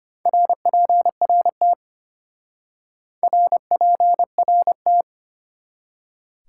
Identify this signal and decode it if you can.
Morse code, heard as a keyed tone, about 32 wpm, 700 Hz. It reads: RPRT RPRT